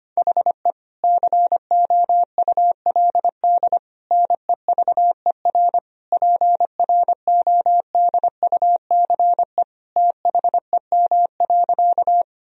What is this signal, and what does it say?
Morse code, heard as a keyed tone, about 25 wpm, 700 Hz. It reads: HE COULD NEVER PRODUCE THEM.
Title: Sentences from Top 1000 Words - Review Part 3 - 25wpm